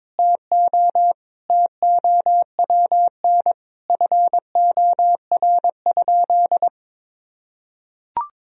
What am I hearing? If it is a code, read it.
Morse code, heard as a keyed tone, about 22 wpm, 700 Hz. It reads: TO TOWN FOR?
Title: Sentences from Top 1000 Words - Review Part 2 - 22wpm